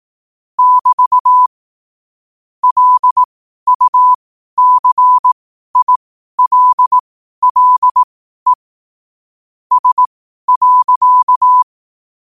Morse code, heard as a keyed tone, about 18 wpm, spaced as for 12 wpm, 1000 Hz. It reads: <BT> LUCILLE S.
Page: Quotes for Wed, 13 Aug 2025 in Morse Code at 12 words per minute.